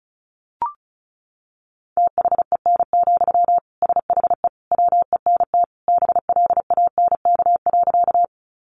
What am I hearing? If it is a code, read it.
Morse code, heard as a keyed tone, about 34 wpm, 700 Hz. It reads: THEN, SHE WENT BLANK.
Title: Sentences from Top 1500 Words 35wpm